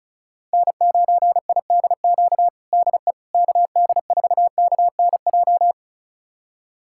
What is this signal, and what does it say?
Morse code, heard as a keyed tone, about 36 wpm, 700 Hz. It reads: N9IDQ DE KD4KNJ
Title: USA Callsign DE Callsign 35wpm